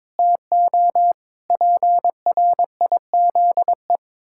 Morse code, heard as a keyed tone, about 22 wpm, 700 Hz. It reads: TO PRIZE
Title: Pangram Challenge v2 - 22wpm